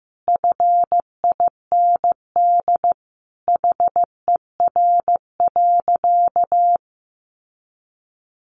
Morse code, heard as a keyed tone, about 14 wpm, 700 Hz. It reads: FIND HER.
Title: Sentences from Top 1300 Words 15wpm